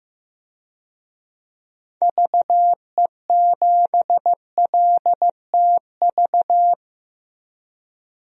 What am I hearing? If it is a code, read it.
Morse code, heard as a keyed tone, about 15 wpm, 700 Hz. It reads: VE7LTV
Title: DXCC-100 Call Signs - 6 Characters - 15wpm